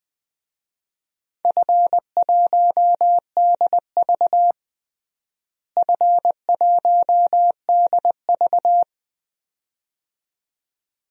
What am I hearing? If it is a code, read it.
Morse code, heard as a keyed tone, about 20 wpm, 700 Hz. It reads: F1DV F1DV
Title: DXCC-100 Call Signs - 4 Characters - Repeated 2x - 20wpm